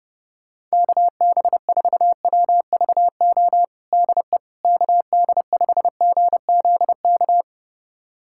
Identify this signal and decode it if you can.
Morse code, heard as a keyed tone, about 32 wpm, 700 Hz. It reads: KB4WVO DE KD5GZK
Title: USA Callsign DE Callsign 30wpm